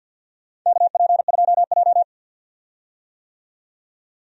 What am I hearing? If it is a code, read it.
Morse code, heard as a keyed tone, about 47 wpm, 700 Hz. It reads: KP2J
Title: DXCC-100 Call Signs - 4 Characters - 50wpm